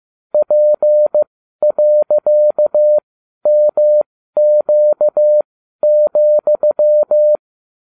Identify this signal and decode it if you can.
Morse code, heard as a keyed tone, about 15 wpm, spaced as for 12 wpm, 600 Hz. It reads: P.MQ,